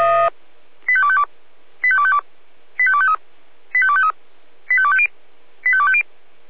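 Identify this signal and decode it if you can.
Five-tone selective call (ZVEI), heard as quick runs of five tones, tones of about 70 ms. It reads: 85252 85252 85253 85253 85250 85250
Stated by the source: ZVEI five-tone selective call received on an SDR.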